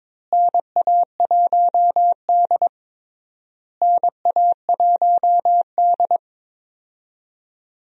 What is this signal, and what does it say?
Morse code, heard as a keyed tone, about 22 wpm, 700 Hz. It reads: NA1D NA1D
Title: USA Call Signs - 4 Characters - Repeated 3x - 22wpm